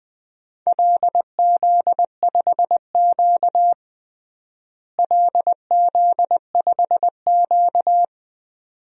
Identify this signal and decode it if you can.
Morse code, heard as a keyed tone, about 20 wpm, 700 Hz. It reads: LZ5Q LZ5Q
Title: DXCC-100 Call Signs - 4 Characters - Repeated 2x - 20wpm